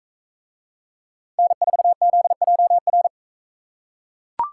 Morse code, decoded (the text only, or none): N4ZJR